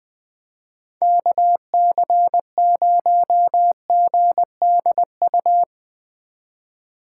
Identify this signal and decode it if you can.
Morse code, heard as a keyed tone, about 24 wpm, 700 Hz. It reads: KC0GDU
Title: Speed Racing - USA 2x3 Call Signs 20wpm